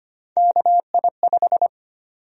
Morse code, heard as a keyed tone, about 25 wpm, 700 Hz. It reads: KI5
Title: USA Call Sign Prefixes - 25wpm